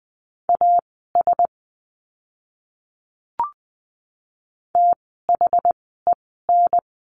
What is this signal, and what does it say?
Morse code, heard as a keyed tone, about 20 wpm, spaced as for 14 wpm, 700 Hz. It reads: AS THEN
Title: 244 Lesson 39f - Letter z - Words 20wpm